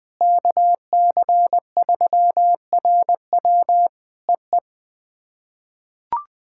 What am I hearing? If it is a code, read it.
Morse code, heard as a keyed tone, about 20 wpm, 700 Hz. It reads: KC3RW EE
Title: POTA QSO Practice - 20wpm